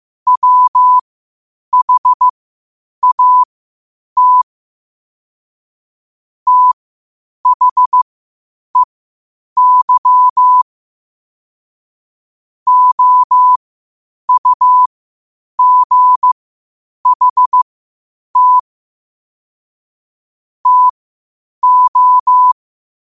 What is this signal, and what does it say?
Morse code, heard as a keyed tone, about 15 wpm, spaced as for 8 wpm, 1000 Hz. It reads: WHAT THEY OUGHT TO
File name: qotdfiles / 8wpm